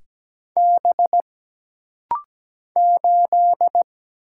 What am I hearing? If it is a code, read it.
Morse code, heard as a keyed tone, about 17 wpm, 700 Hz. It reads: B 8
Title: Single Letter-Number Alt - Rapid-Fire 17wpm